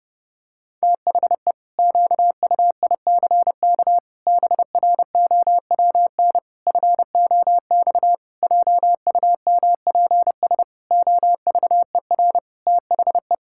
Morse code, heard as a keyed tone, about 30 wpm, 700 Hz. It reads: THE QUICK BROWN FOX JUMPS OVER THE
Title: Pangram Challenge v2 - 30wpm